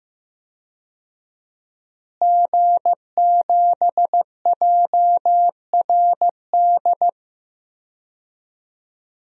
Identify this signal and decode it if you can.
Morse code, heard as a keyed tone, about 15 wpm, 700 Hz. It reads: G7JRD
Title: DXCC-100 Call Signs - 5 Characters - Repeated 3x - 15wpm